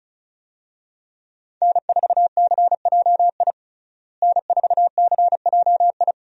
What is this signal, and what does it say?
Morse code, heard as a keyed tone, about 34 wpm, 700 Hz. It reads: N4CJI N4CJI
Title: USA Call Signs - 5 Characters - Repeated 2x - 35wpm